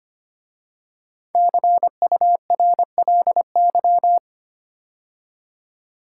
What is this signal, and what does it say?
Morse code, heard as a keyed tone, about 25 wpm, 700 Hz. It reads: CURLY
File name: 5 Letter Words - First 3 Letters Given - 25wpm